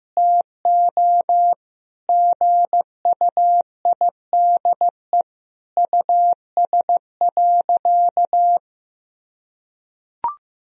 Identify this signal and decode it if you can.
Morse code, heard as a keyed tone, about 15 wpm, 700 Hz. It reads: TO GUIDE US.
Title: Sentences from Top 1000 Words - Review Part 3 - 15wpm